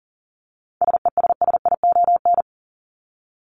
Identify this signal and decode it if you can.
Morse code, heard as a keyed tone, about 40 wpm, 700 Hz. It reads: SESSION
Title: Top 1501-1750 Words 40wpm